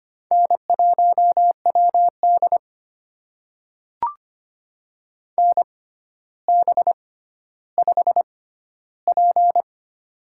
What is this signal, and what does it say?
Morse code, heard as a keyed tone, about 25 wpm, spaced as for 13 wpm, 700 Hz. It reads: N1WD NB5P
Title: USA Call Signs - 4 Characters - 6x Character Spacing - 25wpm